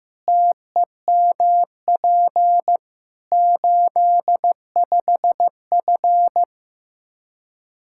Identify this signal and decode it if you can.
Morse code, heard as a keyed tone, about 15 wpm, 700 Hz. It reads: TEMP 85F
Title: Practice QSO - Exchange 2 - Info - 15wpm